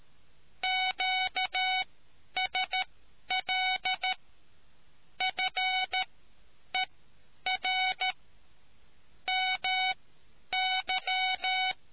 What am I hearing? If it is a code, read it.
Morse code, heard as a keyed tone, about 13 wpm, spaced as for 9 wpm, 750 Hz. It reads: QSL FER MY